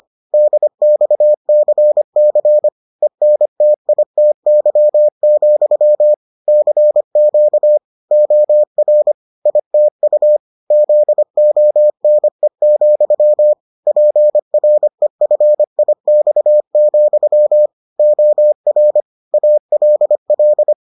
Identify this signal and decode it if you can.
Morse code, heard as a keyed tone, about 25 wpm, 600 Hz. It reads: DXCC ENTITY, CQ OR ITU ZONE, PREFIX, OR ALL